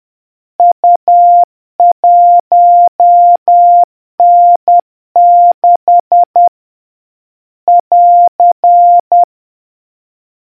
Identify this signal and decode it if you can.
Morse code, heard as a keyed tone, about 10 wpm, 700 Hz. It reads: U1N6 <AR>